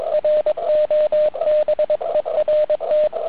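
Morse code, heard as a keyed tone, about 22 wpm, 620 Hz. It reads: ROBERT
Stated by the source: Weak pile-up on 80m answering a UBA fox station, worked with an ATS-3B running about 4W; callers from DL, G, ON.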